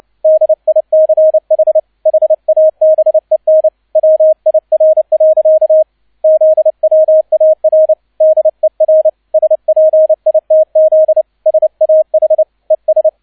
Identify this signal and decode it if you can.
Morse code, heard as a keyed tone, about 29 wpm, 630 Hz, with radio noise. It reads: DICH HABEN WIR. ZWAR DER SPITZ SAH ES
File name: 002 Zweiter Streich 30wpm